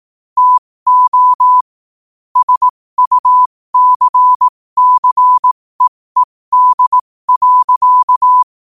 Morse code, heard as a keyed tone, about 18 wpm, spaced as for 15 wpm, 1000 Hz. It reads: TO SUCCEED.
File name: qotdfiles / 15wpm